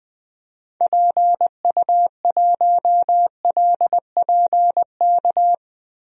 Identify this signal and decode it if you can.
Morse code, heard as a keyed tone, about 20 wpm, 700 Hz. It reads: PU1LPK
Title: DXCC-100 Call Signs - 6 Characters - 20wpm